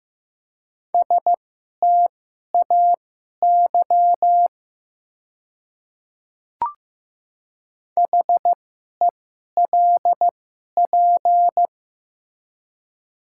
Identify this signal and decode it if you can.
Morse code, heard as a keyed tone, about 15 wpm, spaced as for 11 wpm, 700 Hz. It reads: STAY HELP
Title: 2x Character Spacing - Top 500 Words - 15wpm